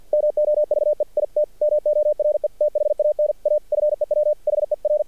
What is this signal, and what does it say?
Morse code, heard as a keyed tone, about 54 wpm, 600 Hz. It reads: MOVE IT MORE THAN A FEW HER
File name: archived / CRSnet